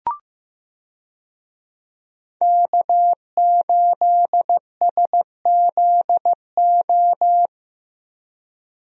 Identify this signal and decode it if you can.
Morse code, heard as a keyed tone, about 15 wpm, 700 Hz. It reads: K8SZO
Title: USA Call Signs - 5 Characters - 15wpm